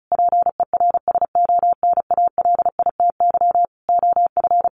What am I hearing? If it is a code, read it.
Morse code, heard as a keyed tone, about 34 wpm, 700 Hz. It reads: PERSONALITY OF
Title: Sentences from Top 1400 Words 35wpm